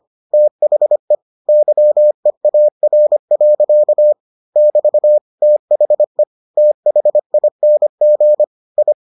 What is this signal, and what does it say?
Morse code, heard as a keyed tone, about 25 wpm, 600 Hz. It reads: THE YEAR. <BT> THE THING I